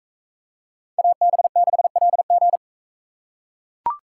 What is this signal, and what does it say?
Morse code, heard as a keyed tone, about 40 wpm, 700 Hz. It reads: AB6LG